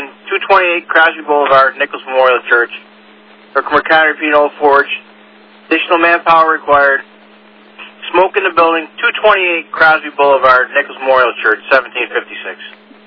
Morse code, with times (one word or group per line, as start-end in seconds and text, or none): none